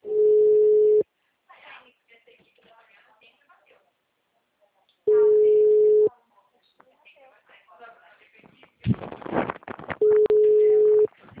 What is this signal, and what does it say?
Ringback tone (Europe, 425 Hz), with speech.